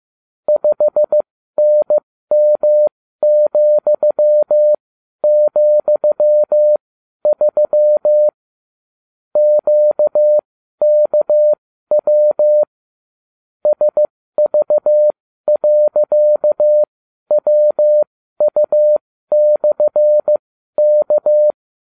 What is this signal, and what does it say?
Morse code, heard as a keyed tone, about 15 wpm, spaced as for 12 wpm, 600 Hz. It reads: 5NM,,3 QKW SV.WU/K